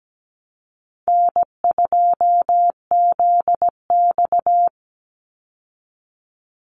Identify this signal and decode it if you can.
Morse code, heard as a keyed tone, about 17 wpm, 700 Hz. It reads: N2ZX
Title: Call Signs Repeated 2x - 17wpm